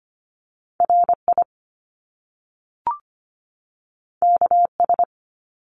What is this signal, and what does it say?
Morse code, heard as a keyed tone, about 25 wpm, 700 Hz. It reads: RI KS
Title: US States 25wpm